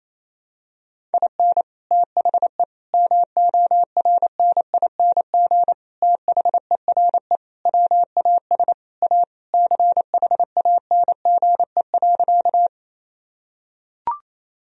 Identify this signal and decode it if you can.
Morse code, heard as a keyed tone, about 28 wpm, 700 Hz. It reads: IN THE MORNING THERE WAS A CHANGE.
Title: Sentences from Top 1000 Words - Review Part 1 - 28wpm